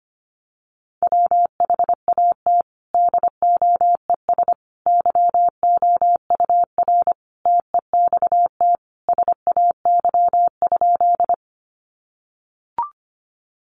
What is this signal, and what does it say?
Morse code, heard as a keyed tone, about 25 wpm, 700 Hz. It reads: WHAT DOES YOUR TEXT SAY?
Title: Sentences from Top 1200 Words 25wpm